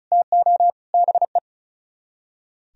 Morse code, heard as a keyed tone, about 46 wpm, 700 Hz. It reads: TO BE